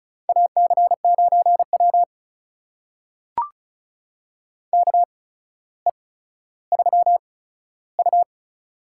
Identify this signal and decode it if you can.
Morse code, heard as a keyed tone, about 36 wpm, spaced as for 15 wpm, 700 Hz. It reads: AC9W KE3U
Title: USA Call Signs - 4 Characters - 8x Character Spacing - 35wpm